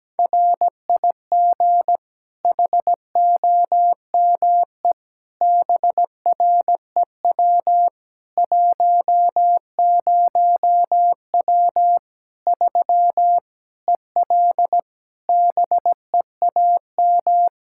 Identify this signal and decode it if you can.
Morse code, heard as a keyed tone, about 17 wpm, 700 Hz. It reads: RIG HOME BREW 10W 3 EL BEAM
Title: K1USN SST QSO Practice 17wpm